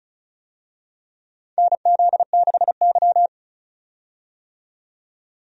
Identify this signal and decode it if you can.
Morse code, heard as a keyed tone, about 36 wpm, 700 Hz. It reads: NZ6Y
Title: USA Call Signs - 4 Characters - 35wpm